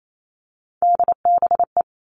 Morse code, heard as a keyed tone, about 28 wpm, 700 Hz. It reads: DBE
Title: All Cut Numbers - 3-Digit 28wpm